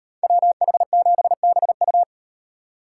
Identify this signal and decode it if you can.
Morse code, heard as a keyed tone, about 38 wpm, 700 Hz. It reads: WH7BU